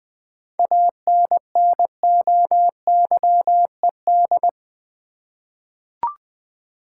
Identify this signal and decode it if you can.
Morse code, heard as a keyed tone, about 20 wpm, 700 Hz. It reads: ANNOYED
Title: 7 Letter Words - First 3 Letters Given - 20wpm